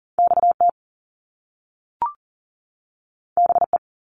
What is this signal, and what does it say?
Morse code, heard as a keyed tone, about 38 wpm, 700 Hz. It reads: XT BE
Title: Top Two Letter Combinations in Words 40wpm